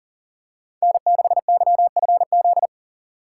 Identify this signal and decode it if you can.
Morse code, heard as a keyed tone, about 40 wpm, 700 Hz. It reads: N6YFZ
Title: USA Call Signs - 2x Character Spacing - 40wpm